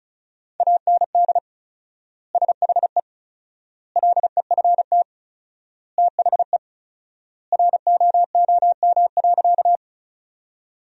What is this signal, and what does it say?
Morse code, heard as a keyed tone, about 34 wpm, 700 Hz. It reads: AND SHE LEFT THE ROOM.
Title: 4x Word-Spacing - Sentences from Top 300 Words - 35wpm